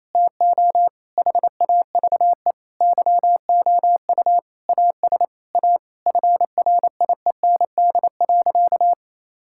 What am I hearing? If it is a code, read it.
Morse code, heard as a keyed tone, about 28 wpm, 700 Hz. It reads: TO HAVE YOU AS A FRIEND.
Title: Sentences from Top 1000 Words - Review Part 6 - 28wpm